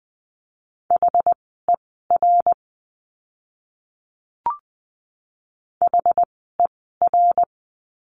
Morse code, heard as a keyed tone, about 20 wpm, spaced as for 14 wpm, 700 Hz. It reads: HER HER
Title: 206 Lesson 33f - Letter j - Words 20wpm